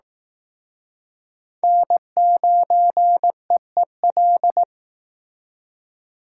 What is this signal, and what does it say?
Morse code, heard as a keyed tone, about 18 wpm, 700 Hz. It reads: N9EEL
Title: Speed Racing - USA 1x3 Call Signs 15wpm